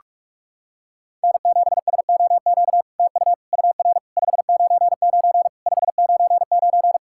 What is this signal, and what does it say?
Morse code, heard as a keyed tone, about 46 wpm, 700 Hz. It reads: N7SOQ TU UR 599 599
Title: SOTA QSO Practice 45wpm